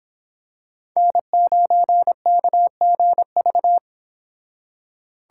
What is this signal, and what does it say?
Morse code, heard as a keyed tone, about 25 wpm, 700 Hz. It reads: N9KGV